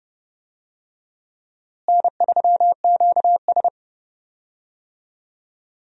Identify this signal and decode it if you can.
Morse code, heard as a keyed tone, about 30 wpm, 700 Hz. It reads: N3QS
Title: USA Call Signs - 4 Characters - Repeated 3x - 30wpm